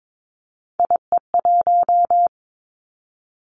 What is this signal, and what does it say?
Morse code, heard as a keyed tone, about 22 wpm, 700 Hz. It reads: IE1
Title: Three Challenging Letter-Number Combinations 22wpm